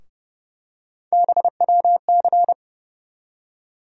Morse code, heard as a keyed tone, about 32 wpm, 700 Hz. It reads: BWC
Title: USA Call Sign Postfixes - 3 Letters 30wpm